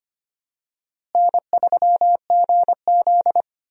Morse code, heard as a keyed tone, about 25 wpm, 700 Hz. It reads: N3GZ
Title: USA Call Signs - 4 Characters - Repeated 3x - 25wpm